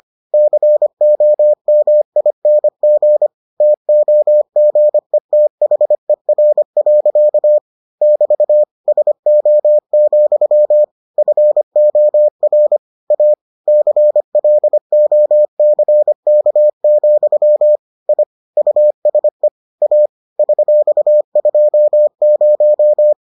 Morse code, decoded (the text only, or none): COMING TOGETHER. <BT> SO, FOR A CLOCK, I USE A $20